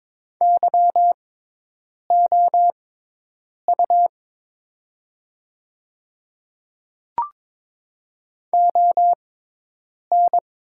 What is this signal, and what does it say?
Morse code, heard as a keyed tone, about 22 wpm, spaced as for 8 wpm, 700 Hz. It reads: YOU ON
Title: 6x Character Spacing - Top 100 Words - 22wpm